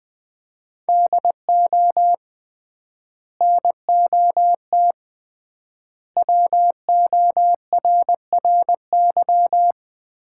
Morse code, heard as a keyed tone, about 20 wpm, 700 Hz. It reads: DO NOT WORRY
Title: 3x Word-Spacing - Sets of 5 Words - 20wpm